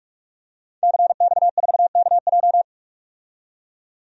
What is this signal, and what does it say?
Morse code, heard as a keyed tone, about 41 wpm, 700 Hz. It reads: CX4KJ
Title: DXCC-100 Call Signs - 5 Characters - 45wpm